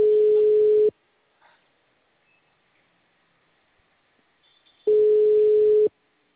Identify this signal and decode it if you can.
Ringback tone (Europe, 425 Hz).